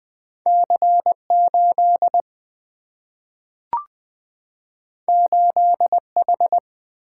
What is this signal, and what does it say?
Morse code, heard as a keyed tone, about 20 wpm, 700 Hz. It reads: C8 8H
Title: Single-Letter Single-Number 20wpm